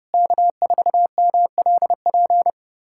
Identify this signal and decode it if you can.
Morse code, heard as a keyed tone, about 32 wpm, 700 Hz. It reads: K4MLP
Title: USA Callsign DE Callsign - 5 Characters - 30wpm